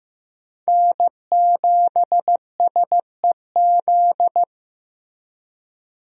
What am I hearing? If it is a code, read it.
Morse code, heard as a keyed tone, about 15 wpm, 700 Hz. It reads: N7SEZ
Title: USA Call Signs - 5 Characters - Repeated 2x - 15wpm